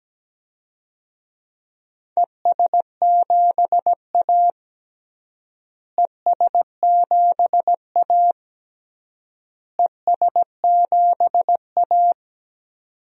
Morse code, heard as a keyed tone, about 17 wpm, 700 Hz. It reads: ES7A ES7A ES7A
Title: DXCC-100 Call Signs - 4 Characters - Repeated 3x - 17wpm